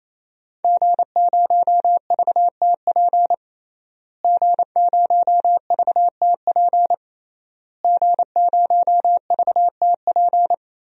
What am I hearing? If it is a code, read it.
Morse code, heard as a keyed tone, about 27 wpm, 700 Hz. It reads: G0VTP G0VTP G0VTP
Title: DXCC-100 Call Signs - 5 Characters - Repeated 3x - 28wpm